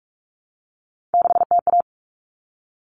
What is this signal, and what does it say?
Morse code, heard as a keyed tone, about 44 wpm, 700 Hz. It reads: 6TA
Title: Common Cut Numbers - 3-Digit 45wpm